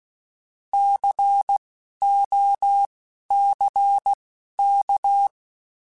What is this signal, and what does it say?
Morse code, heard as a keyed tone, about 16 wpm, spaced as for 12 wpm, 800 Hz. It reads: COCK
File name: Alphabetical / Alphabetical mp3s / Cs @10